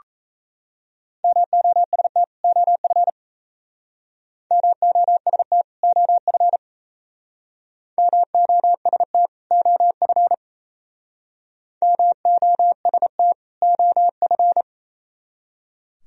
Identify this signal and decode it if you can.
Morse code, heard as a keyed tone, about 35 wpm, 700 Hz. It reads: MOST OF MOST OF MOST OF MOST OF
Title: Speed-Racing - Sets of 2 Words 28wpm